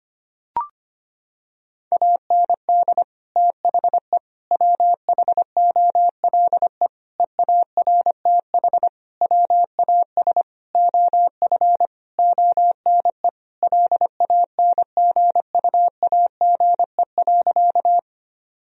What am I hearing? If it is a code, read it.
Morse code, heard as a keyed tone, about 25 wpm, 700 Hz. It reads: AND THE WHOLE EARTH WAS OF ONE LANGUAGE.
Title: Sentences from Top 1000 Words - Review Part 3 - 25wpm